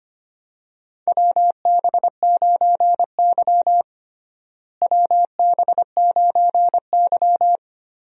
Morse code, heard as a keyed tone, about 25 wpm, 700 Hz. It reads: WB9Y WB9Y
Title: USA Call Signs - 4 Characters - Repeated 3x - 25wpm